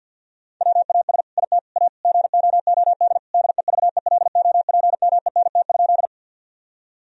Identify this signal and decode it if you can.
Morse code, heard as a keyed tone, about 50 wpm, 700 Hz. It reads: WAS IT A GOOD DEVELOPMENT?
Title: Sentences from Top 1000 Words - Review Part 4 - 50wpm